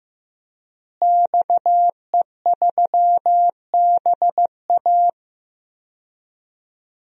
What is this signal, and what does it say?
Morse code, heard as a keyed tone, about 15 wpm, 700 Hz. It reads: XE3BA
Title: DXCC-100 Call Signs - 5 Characters - Repeated 3x - 15wpm